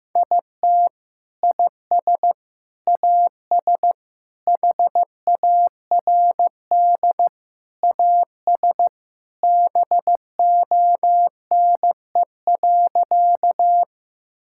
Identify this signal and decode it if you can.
Morse code, heard as a keyed tone, about 15 wpm, 700 Hz. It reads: IT IS AS HARD AS BONE.
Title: Sentences from Top 1600 Words 15wpm